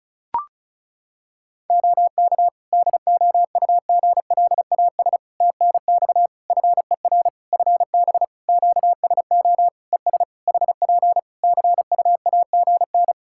Morse code, heard as a keyed tone, about 36 wpm, 700 Hz. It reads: OK DOUGLAS TNX FER FB QSO ES HP CUAGN
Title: Practice QSO - Exchange 4 - Ending - 35wpm